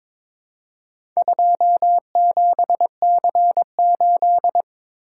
Morse code, heard as a keyed tone, about 22 wpm, 700 Hz. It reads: 27C8
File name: practice / 22wpm / Four Letter-Number Combinations - 22wpm